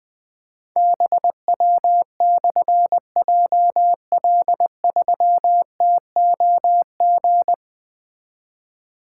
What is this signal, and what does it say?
Morse code, heard as a keyed tone, about 20 wpm, 700 Hz. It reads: BW/JL3TOG
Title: Difficult Call Signs 20wpm